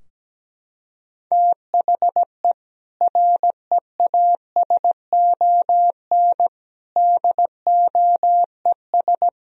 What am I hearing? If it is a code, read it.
Morse code, heard as a keyed tone, about 17 wpm, 700 Hz. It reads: THE REASON DOES
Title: Sentences from Top 1000 Words - Review Part 3 - 17wpm